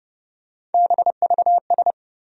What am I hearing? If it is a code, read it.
Morse code, heard as a keyed tone, about 30 wpm, 700 Hz. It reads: BVS
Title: USA Call Sign Postfixes - 3 Letters 30wpm